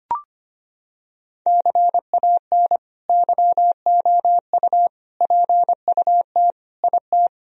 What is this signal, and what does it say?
Morse code, heard as a keyed tone, about 25 wpm, 700 Hz. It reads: CAN YOU PUT IT
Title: Sentences from Top 1000 Words - Review Part 4 - 25wpm